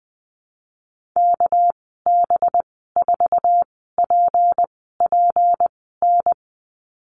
Morse code, heard as a keyed tone, about 20 wpm, spaced as for 14 wpm, 700 Hz. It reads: KB4PPN
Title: 213 Lesson 34g - Number 8 - Call Signs 20wpm